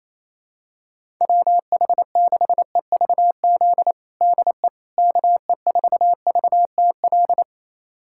Morse code, heard as a keyed tone, about 28 wpm, 700 Hz. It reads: WH6EVZ DE KE4VTL
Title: USA Callsign DE Callsign - 6 Characters - 28wpm